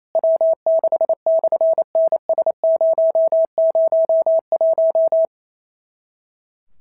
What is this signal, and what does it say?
Morse code, heard as a keyed tone, about 28 wpm, 650 Hz. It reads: W6/NS001
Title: SOTA QSO Practice 28wpm